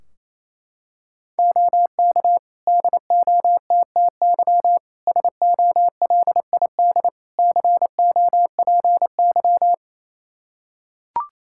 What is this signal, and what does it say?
Morse code, heard as a keyed tone, about 28 wpm, 700 Hz. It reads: OK DOTTY SOLID COPY
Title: K1USN SST QSO Practice 28wpm